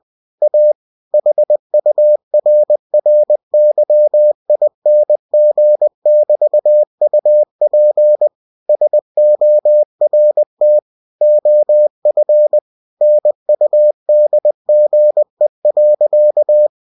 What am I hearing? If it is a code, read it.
Morse code, heard as a keyed tone, about 20 wpm, 600 Hz. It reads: A HURRYING<BT>UP SORT OF NUDGE.
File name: cw-books / winnie-the-pooh / winnie-20wpm-page-0068